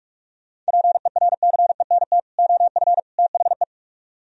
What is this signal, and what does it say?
Morse code, heard as a keyed tone, about 38 wpm, 700 Hz. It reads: PERCENT OF THE
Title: Speed-Racing - Sets of 3 Words 30wpm